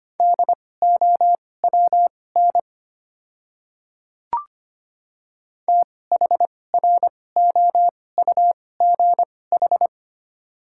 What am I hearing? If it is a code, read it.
Morse code, heard as a keyed tone, about 25 wpm, spaced as for 18 wpm, 690 Hz. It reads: DOWN THROUGH
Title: 2x Character Spacing - Top 300 Words - 25wpm